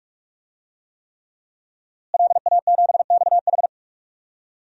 Morse code, heard as a keyed tone, about 46 wpm, 700 Hz. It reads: LA7XH
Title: DXCC-100 Call Signs - 5 Characters - Repeated 3x - 45wpm